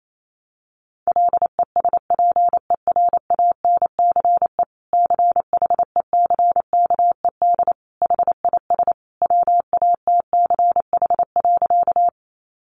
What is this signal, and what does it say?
Morse code, heard as a keyed tone, about 28 wpm, 700 Hz. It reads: LESPERANCE CHECKED HIS WATCH.
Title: A Sound Of Thunder 28wpm